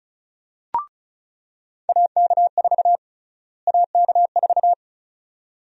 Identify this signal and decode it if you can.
Morse code, heard as a keyed tone, about 36 wpm, 700 Hz. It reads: AK4 AK4
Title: USA Call Sign Prefixes - Repeated 2x - 35wpm